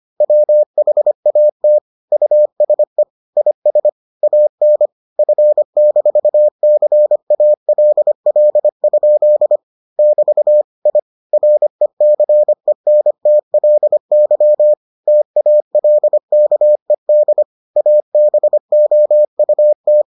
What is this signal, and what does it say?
Morse code, heard as a keyed tone, about 25 wpm, 600 Hz. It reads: WHAT USE IS AN F-CALL? <BT> I RECENTLY TALKED ABOUT